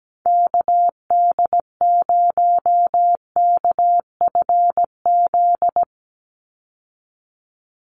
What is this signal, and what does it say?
Morse code, heard as a keyed tone, about 17 wpm, 700 Hz. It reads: KD0KFZ
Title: USA Call Signs 17wpm